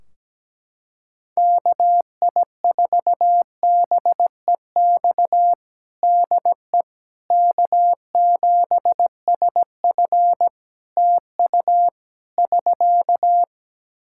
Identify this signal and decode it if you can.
Morse code, heard as a keyed tone, about 17 wpm, 700 Hz. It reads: KI4BEX DE K7SF TU <SK>
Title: Practice QSO - Exchange 4 - Ending - 17wpm